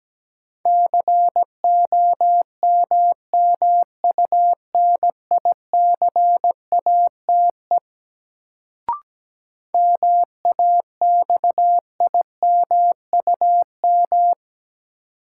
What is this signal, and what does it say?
Morse code, heard as a keyed tone, about 17 wpm, 700 Hz. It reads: COMMUNICATE MAXIMUM
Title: Sentences from Top 1300 Words - New Word Drill Alt - 17wpm